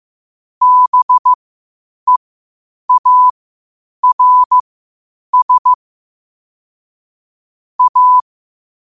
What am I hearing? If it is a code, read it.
Morse code, heard as a keyed tone, about 15 wpm, spaced as for 8 wpm, 1000 Hz. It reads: BEARS A
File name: qotdfiles / 8wpm